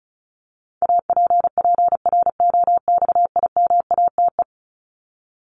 Morse code, heard as a keyed tone, about 34 wpm, 700 Hz. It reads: APPROXIMATE
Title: LY Suffix - Top 100 Words with Base Words 35wpm